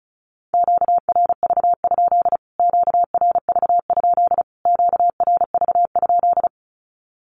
Morse code, heard as a keyed tone, about 34 wpm, 700 Hz. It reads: QRV? QRV? QRV?
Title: Q-Codes Repeated 3x 35wpm